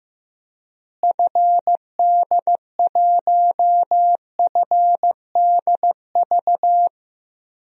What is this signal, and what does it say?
Morse code, heard as a keyed tone, about 16 wpm, 700 Hz. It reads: FD1FDV
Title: DXCC-100 Call Signs - 6 Characters - Repeated 3x - 15wpm